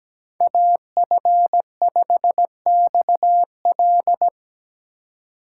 Morse code, heard as a keyed tone, about 20 wpm, 700 Hz. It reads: AF5XL
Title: Speed Racing - USA 2x2 Call Signs 17wpm